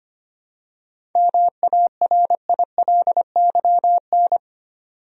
Morse code, heard as a keyed tone, about 25 wpm, 700 Hz. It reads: MARILYN
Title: USA Common Names - 25wpm